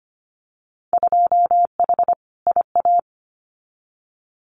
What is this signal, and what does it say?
Morse code, heard as a keyed tone, about 25 wpm, 700 Hz. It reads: 2H IA
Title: Winter Field Day Exchanges Encore 25wpm